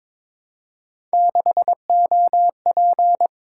Morse code, heard as a keyed tone, about 22 wpm, 700 Hz. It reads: 6OP
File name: Three Letter-Number Combinations - 22wpm